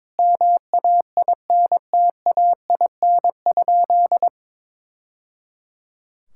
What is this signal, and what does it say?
Morse code, heard as a keyed tone, about 22 wpm, 700 Hz. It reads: MAINTAIN?
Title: Sentences from Top 1000 Words - Review Part 5 - 22wpm